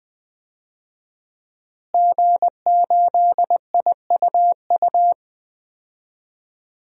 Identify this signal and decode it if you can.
Morse code, heard as a keyed tone, about 20 wpm, 700 Hz. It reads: G8IUU
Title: DXCC-100 Call Signs - 5 Characters - Repeated 2x - 20wpm